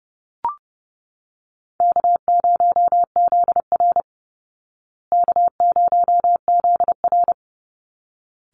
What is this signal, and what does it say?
Morse code, heard as a keyed tone, about 30 wpm, 700 Hz. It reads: K0ZR K0ZR
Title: Call Signs Repeated 3x - 30wpm